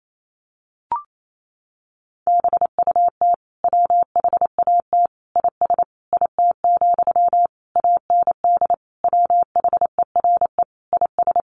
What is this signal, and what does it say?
Morse code, heard as a keyed tone, about 28 wpm, 700 Hz. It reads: BUT WHAT IS IT, AND WHERE IS
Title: Sentences from Top 300 Words 28wpm